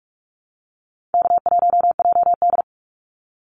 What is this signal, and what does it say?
Morse code, heard as a keyed tone, about 46 wpm, 700 Hz. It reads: K1JD
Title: NA SOTA Activator Callsigns 45wpm